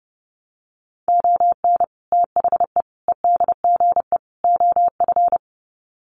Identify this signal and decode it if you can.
Morse code, heard as a keyed tone, about 30 wpm, 700 Hz. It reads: ON THE EDGE OF
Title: Sets of 4 Words 30wpm